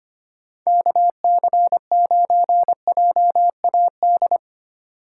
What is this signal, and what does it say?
Morse code, heard as a keyed tone, about 25 wpm, 700 Hz. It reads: KC9JAD
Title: USA Call Signs - 4x Character Spacing - 25wpm